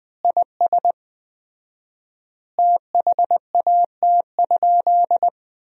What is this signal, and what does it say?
Morse code, heard as a keyed tone, about 20 wpm, 700 Hz. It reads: IS THAT?
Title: 4x Word-Spacing - Sentences from Top 100 Words - 20wpm